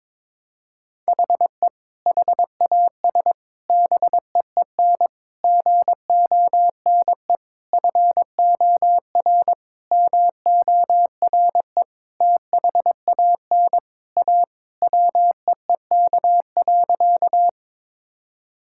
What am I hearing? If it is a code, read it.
Morse code, heard as a keyed tone, about 22 wpm, 700 Hz. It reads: HE HAS BEEN GONE FOR MORE THAN A WEEK.
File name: practice / 22wpm / Sentences with 4-Word Phrases - 22wpm